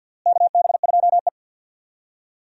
Morse code, heard as a keyed tone, about 52 wpm, 700 Hz. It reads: KB2E
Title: USA Call Signs - 4 Characters - Repeated 3x - 50wpm